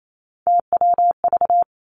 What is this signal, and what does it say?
Morse code, heard as a keyed tone, about 28 wpm, 700 Hz. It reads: TWV
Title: Three Challenging Letter-Number Combinations 28wpm